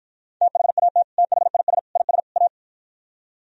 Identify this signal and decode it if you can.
Morse code, heard as a keyed tone, about 50 wpm, 700 Hz. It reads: THAT THIS IS A